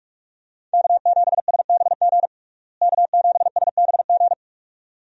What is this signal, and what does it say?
Morse code, heard as a keyed tone, about 41 wpm, 700 Hz. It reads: K7SBG K7SBG
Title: USA Call Signs - 5 Characters - Repeated 2x - 45wpm